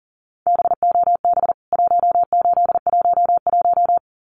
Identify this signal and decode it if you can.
Morse code, heard as a keyed tone, about 40 wpm, 700 Hz. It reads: BOB 1811
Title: CWT Exchange 40wpm